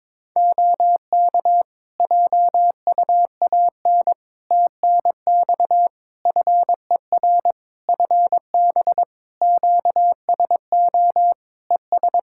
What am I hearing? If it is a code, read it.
Morse code, heard as a keyed tone, about 22 wpm, 700 Hz. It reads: OK JUAN TNX FER FB QSO ES
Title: Practice QSO - Exchange 4 - Ending - 22wpm